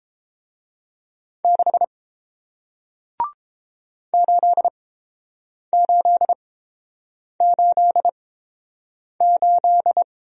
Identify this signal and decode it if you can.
Morse code, heard as a keyed tone, about 29 wpm, spaced as for 7 wpm, 690 Hz. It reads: 6 8888